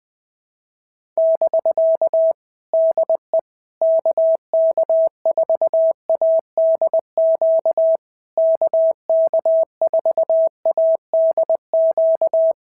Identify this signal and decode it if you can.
Morse code, heard as a keyed tone, about 20 wpm, 650 Hz. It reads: <BK> DE KK4ADQ KK4ADQ
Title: POTA QSO Practice - 20wpm